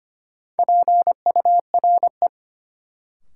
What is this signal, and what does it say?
Morse code, heard as a keyed tone, about 25 wpm, 700 Hz. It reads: PURE
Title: Sentences from Top 1500 Words - New Word Drill - 25wpm